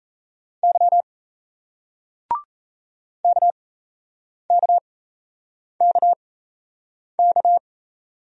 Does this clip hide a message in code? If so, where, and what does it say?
Yes: Morse code from 0.6 to 7.6 s, heard as a keyed tone, about 38 wpm, spaced as for 8 wpm, 700 Hz. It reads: Y KKKK